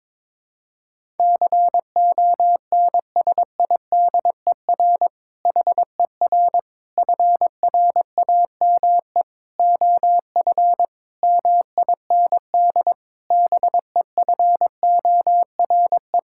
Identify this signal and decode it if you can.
Morse code, heard as a keyed tone, about 21 wpm, 700 Hz. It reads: CONSIDER HER FRAME OF MIND BEFORE
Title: Sentences from Top 1000 Words - Review Part 6 - 22wpm